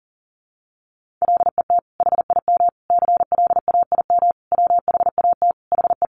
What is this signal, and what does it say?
Morse code, heard as a keyed tone, about 40 wpm, 700 Hz. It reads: LET HIM CLAIM WHAT HE
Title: Sentences from Top 1500 Words 40wpm